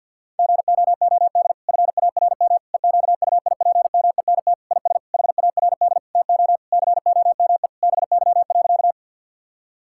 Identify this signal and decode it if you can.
Morse code, heard as a keyed tone, about 50 wpm, 700 Hz. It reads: GOOD FARM EQUIPMENT IS HARD TO COME BY.
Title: Sentences from Top 1000 Words - Review Part 5 - 50wpm